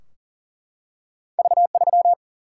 Morse code, heard as a keyed tone, about 40 wpm, 700 Hz. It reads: V 3
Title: Single-Letter Single-Number 40wpm